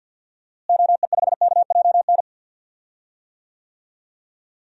Morse code, heard as a keyed tone, about 47 wpm, 700 Hz. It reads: OE5KJN
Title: DXCC-100 Call Signs - 6 Characters - Repeated 3x - 50wpm